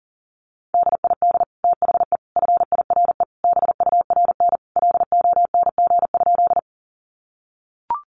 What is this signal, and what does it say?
Morse code, heard as a keyed tone, about 37 wpm, 700 Hz. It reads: DID THE FIRE BURN LONG?
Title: Sentences from Top 1300 Words 40wpm